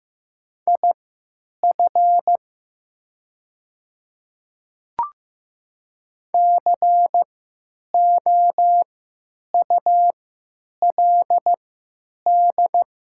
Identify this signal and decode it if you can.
Morse code, heard as a keyed tone, about 15 wpm, spaced as for 9 wpm, 700 Hz. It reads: IF COULD
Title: 3x Character Spacing - Top 300 Words - 15wpm